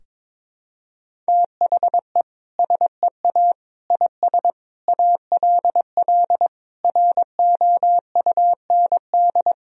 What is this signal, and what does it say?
Morse code, heard as a keyed tone, about 22 wpm, 700 Hz. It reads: THE SEA IS ALL ROUND
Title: Sentences from Top 1000 Words - Review Part 1 - 22wpm